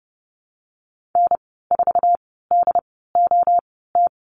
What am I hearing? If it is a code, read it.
Morse code, heard as a keyed tone, about 30 wpm, spaced as for 17 wpm, 700 Hz. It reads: N4DOT
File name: Lesson 14g - Number 2 - Call Signs 30wpm